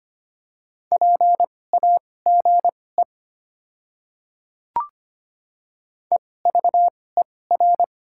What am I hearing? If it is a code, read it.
Morse code, heard as a keyed tone, about 25 wpm, spaced as for 18 wpm, 690 Hz. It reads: PAGE EVER
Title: 2x Character Spacing - Top 500 Words - 25wpm